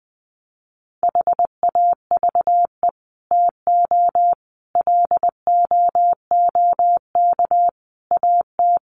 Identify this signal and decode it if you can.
Morse code, heard as a keyed tone, about 20 wpm, 700 Hz. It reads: HAVE TO LOOK AT
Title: Sets of 5 Words Encore 20wpm